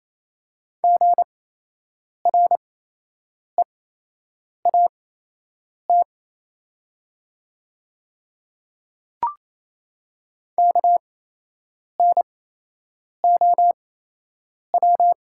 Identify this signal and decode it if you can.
Morse code, heard as a keyed tone, about 28 wpm, spaced as for 8 wpm, 700 Hz. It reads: GREAT KNOW
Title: 8x Character Spacing - Top 500 Words - 28wpm